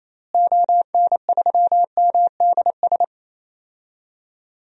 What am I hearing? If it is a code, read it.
Morse code, heard as a keyed tone, about 28 wpm, 700 Hz. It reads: ON3MDS
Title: DXCC-100 Call Signs - 6 Characters - Repeated 2x - 28wpm